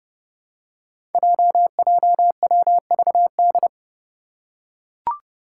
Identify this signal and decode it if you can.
Morse code, heard as a keyed tone, about 30 wpm, 700 Hz. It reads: JJWVD
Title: Code Groups - 5 Letters - 30wpm